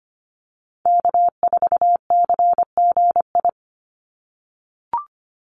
Morse code, heard as a keyed tone, about 25 wpm, 700 Hz. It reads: K4CGI
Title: USA Call Signs 25wpm